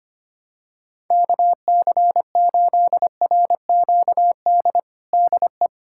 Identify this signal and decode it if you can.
Morse code, heard as a keyed tone, about 25 wpm, 700 Hz. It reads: KC8RQD DE
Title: USA Callsign DE Callsign - 6 Characters - 25wpm